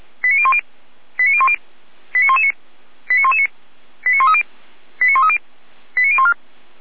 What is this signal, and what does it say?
Five-tone selective call (ZVEI), heard as quick runs of five tones, tones of about 70 ms. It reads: 89010 89010 89119 89119 89130 89130 89014